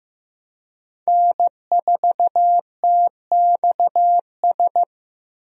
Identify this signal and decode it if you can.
Morse code, heard as a keyed tone, about 15 wpm, 700 Hz. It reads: N4TXS
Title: USA Call Signs - 5 Characters - Repeated 3x - 15wpm